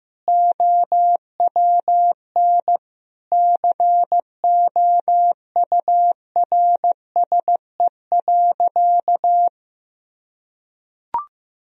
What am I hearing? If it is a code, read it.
Morse code, heard as a keyed tone, about 15 wpm, 700 Hz. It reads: OWN COURSE.
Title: Sentences from Top 1000 Words - Review Part 1 - 15wpm